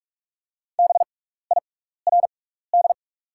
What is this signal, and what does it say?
Morse code, heard as a keyed tone, about 44 wpm, spaced as for 15 wpm, 690 Hz. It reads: BIRD
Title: 6x Character Spacing - Top 500 Words - 45wpm